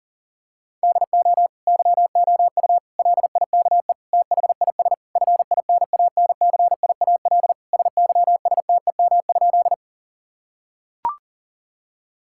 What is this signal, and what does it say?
Morse code, heard as a keyed tone, about 40 wpm, 700 Hz. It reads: DO YOU LIKE THIS FINANCIAL SYSTEM?
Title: Sentences from Top 1000 Words - Review Part 5 - 40wpm